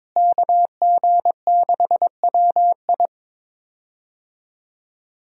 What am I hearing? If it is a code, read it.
Morse code, heard as a keyed tone, about 22 wpm, 700 Hz. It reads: KG6WI
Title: USA Call Signs - 5 Characters - Repeated 3x - 22wpm